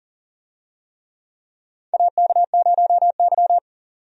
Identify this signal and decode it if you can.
Morse code, heard as a keyed tone, about 39 wpm, 700 Hz. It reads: AK0Y
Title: USA Call Signs - 4 Characters - 40wpm